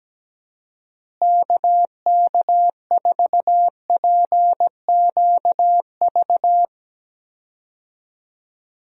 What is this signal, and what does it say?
Morse code, heard as a keyed tone, about 17 wpm, 700 Hz. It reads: KK4PQV
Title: USA Call Signs - 6 Characters - Repeated 2x - 17wpm